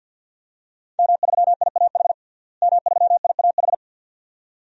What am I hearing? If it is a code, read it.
Morse code, heard as a keyed tone, about 52 wpm, 700 Hz. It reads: M3IAH M3IAH
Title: DXCC-100 Call Signs - 5 Characters - Repeated 2x - 50wpm